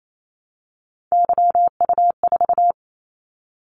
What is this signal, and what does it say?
Morse code, heard as a keyed tone, about 28 wpm, 700 Hz. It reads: YU4
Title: Callsign Prefixes for Top 25 DXCC - 28wpm